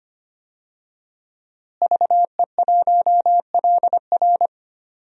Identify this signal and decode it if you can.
Morse code, heard as a keyed tone, about 25 wpm, 700 Hz. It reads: VE1LR
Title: DXCC-100 Call Signs - 5 Characters - Repeated 2x - 25wpm